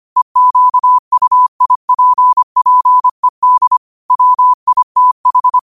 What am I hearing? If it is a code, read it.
Morse code, heard as a keyed tone, about 25 wpm, 1000 Hz. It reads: EQUIPPED WITH
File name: qotdfiles / 25wpm